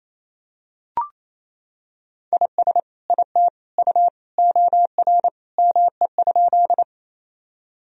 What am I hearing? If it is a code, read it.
Morse code, heard as a keyed tone, about 28 wpm, 700 Hz. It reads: IS IT U OR ME?
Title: Sentences from Top 200 Words and Abbreviations 28wpm